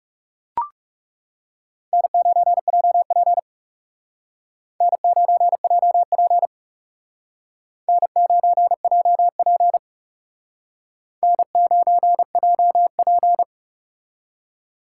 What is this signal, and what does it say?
Morse code, heard as a keyed tone, about 38 wpm, 700 Hz. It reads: N9JP N9JP N9JP N9JP
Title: Speed Racing - USA Call Signs 30wpm